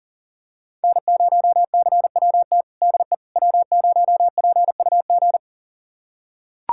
Morse code, heard as a keyed tone, about 40 wpm, 700 Hz. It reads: N0CWT DE W0PUG
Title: USA Callsign DE Callsign - 5 Characters - 40wpm